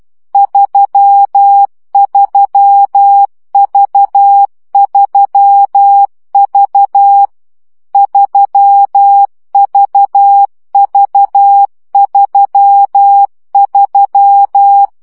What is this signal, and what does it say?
Morse code, heard as a keyed tone, about 12 wpm, 800 Hz. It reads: 33V3V 3VV33
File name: CW gyakorlas v 3 10 baud